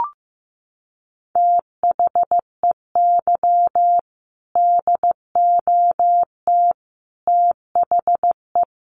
Morse code, heard as a keyed tone, about 15 wpm, 700 Hz. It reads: THEY DOT THE
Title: Sentences from Top 1500 Words 15wpm